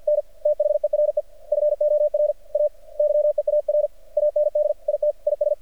Morse code, heard as a keyed tone, about 53 wpm, 600 Hz. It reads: N THERE FOR A YEAR AND IT IS
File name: archived / CRSnet